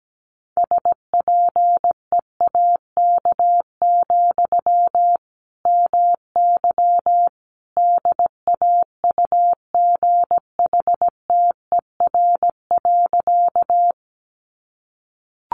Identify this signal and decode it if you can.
Morse code, heard as a keyed tone, about 17 wpm, 700 Hz. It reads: SPEAK, MY DAUGHTER.
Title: Sentences from Top 1500 Words 17wpm